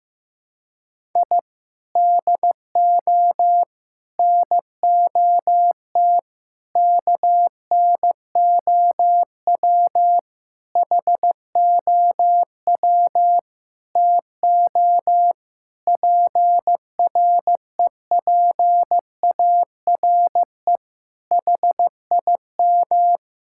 Morse code, heard as a keyed tone, about 15 wpm, 700 Hz. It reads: I DO NOT KNOW HOW TO PREPARE HIM
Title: Sentences from Top 1000 Words - Review Part 6 - 15wpm